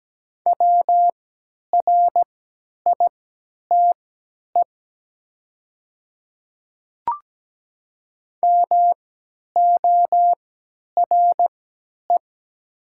Morse code, heard as a keyed tone, about 17 wpm, spaced as for 10 wpm, 700 Hz. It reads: WRITE MORE
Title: 3x Character Spacing - Top 100 Words - 17wpm